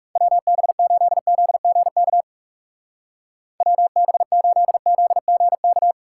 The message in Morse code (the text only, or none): WB8ZGK WB8ZGK